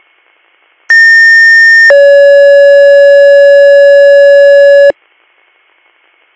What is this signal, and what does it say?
Two-tone page, heard as a one-second tone, then a three-second tone, about 1800 then 560 Hz. It is heard from 0.9 to 4.9 s.